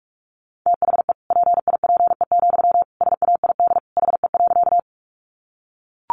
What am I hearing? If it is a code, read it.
Morse code, heard as a keyed tone, about 44 wpm, 700 Hz. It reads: THE PIPE, SAID HE.
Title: Sentences from Top 1500 Words 45wpm